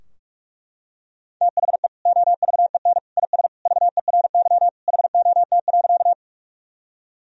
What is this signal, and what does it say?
Morse code, heard as a keyed tone, about 41 wpm, 700 Hz. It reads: THE OVEN IS VERY HOT.
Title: Sentences from Top 1000 Words - Review Part 4 - 45wpm